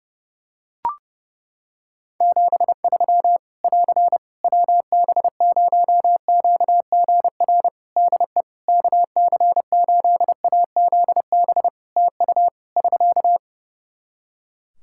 Morse code, heard as a keyed tone, about 30 wpm, 700 Hz. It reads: 73 <AR> WB0QGR DE KC8AZB TU <SK>
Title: Practice QSO - Exchange 4 - Ending - 30wpm